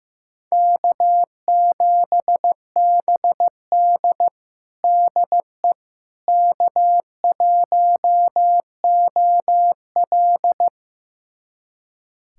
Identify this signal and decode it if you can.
Morse code, heard as a keyed tone, about 15 wpm, 700 Hz. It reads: K7BD DE K1OL
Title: USA Callsign DE Callsign - 4 Characters - 15wpm